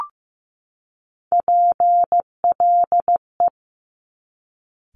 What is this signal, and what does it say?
Morse code, heard as a keyed tone, about 15 wpm, 700 Hz. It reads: PLE
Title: Top Three Letter Combinations in Words 15wpm